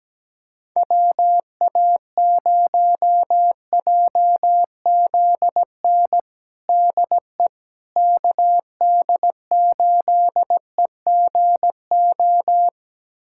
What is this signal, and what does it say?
Morse code, heard as a keyed tone, about 17 wpm, 700 Hz. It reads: WA0JZN DE KD8EGO
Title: USA Callsign DE Callsign 17wpm